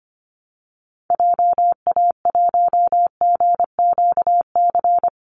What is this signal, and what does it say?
Morse code, heard as a keyed tone, about 25 wpm, 700 Hz. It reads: JA1GQC
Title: Call Signs - 25wpm